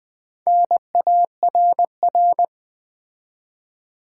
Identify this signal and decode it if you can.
Morse code, heard as a keyed tone, about 20 wpm, 700 Hz. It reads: NARR
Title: Build A Word - 2 Letters At A Time - 6-Letter Words - 20wpm